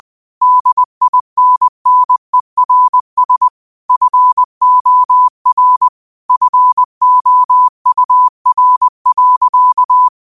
Morse code, heard as a keyed tone, about 20 wpm, 1000 Hz. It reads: DINNERS FOR FOUR.